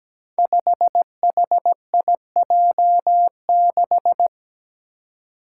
Morse code, heard as a keyed tone, about 17 wpm, 700 Hz. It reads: 5HIJ6
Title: Five Challenging Letter-Number Combinations 17wpm